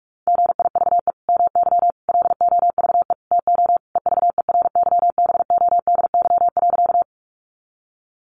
Morse code, heard as a keyed tone, about 44 wpm, 700 Hz. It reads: GIVE MY LOVE TO EVERYBODY.
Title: Sentences from Top 1400 Words 45wpm